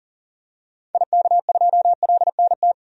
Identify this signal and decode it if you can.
Morse code, heard as a keyed tone, about 40 wpm, 700 Hz. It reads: IK2LNT
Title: DXCC-100 Call Signs - 6 Characters - 40wpm